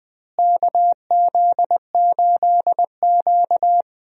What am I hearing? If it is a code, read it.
Morse code, heard as a keyed tone, about 20 wpm, 700 Hz. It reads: KZ8Q
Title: USA Call Signs - 4 Characters - Repeated 2x - 20wpm